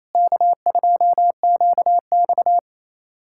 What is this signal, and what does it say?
Morse code, heard as a keyed tone, about 27 wpm, 700 Hz. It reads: K2QX
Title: USA Callsign DE Callsign - 4 Characters - 28wpm